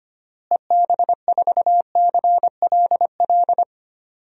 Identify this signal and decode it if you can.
Morse code, heard as a keyed tone, about 25 wpm, 700 Hz. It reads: EB4CLL
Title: DXCC-100 Call Signs - 6 Characters - Repeated 3x - 25wpm